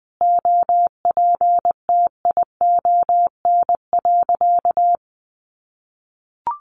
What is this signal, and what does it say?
Morse code, heard as a keyed tone, about 20 wpm, 700 Hz. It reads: OPTION.
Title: Sentences from Top 1300 Words 20wpm